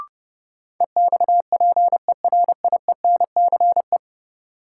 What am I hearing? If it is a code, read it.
Morse code, heard as a keyed tone, about 30 wpm, 700 Hz. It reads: EXPERIENCE
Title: Sentences from Top 600 Words - New Word Drill - 30wpm